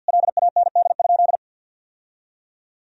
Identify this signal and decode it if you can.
Morse code, heard as a keyed tone, about 52 wpm, 700 Hz. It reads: LAND?
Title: Sentences from Top 1000 Words - Review Part 3 - 50wpm